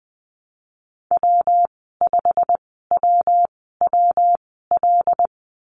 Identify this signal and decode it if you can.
Morse code, heard as a keyed tone, about 20 wpm, spaced as for 14 wpm, 700 Hz. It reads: W5WWL
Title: 154 Lesson 25g - Letter p - Call Signs 20wpm